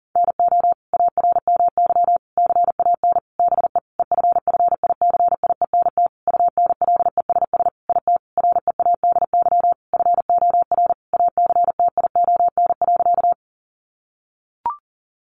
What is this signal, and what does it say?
Morse code, heard as a keyed tone, about 40 wpm, 700 Hz. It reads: NO ARMY CAN BE EFFICIENT UNLESS IT READY FOR ACTION.
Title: Sentences from Top 1300 Words 40wpm